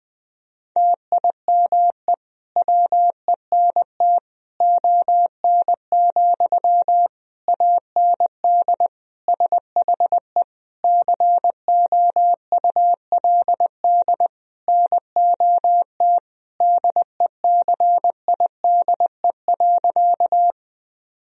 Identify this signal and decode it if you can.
Morse code, heard as a keyed tone, about 20 wpm, 700 Hz. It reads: TIME WENT ON, AND SHE COULD NOT DECIDE.
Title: Sentences from Top 1000 Words - Review Part 3 - 20wpm